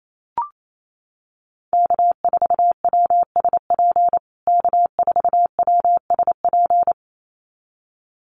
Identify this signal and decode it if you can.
Morse code, heard as a keyed tone, about 28 wpm, 700 Hz. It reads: K4WSP K4WSP
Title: USA Call Signs Repeated 2x 28wpm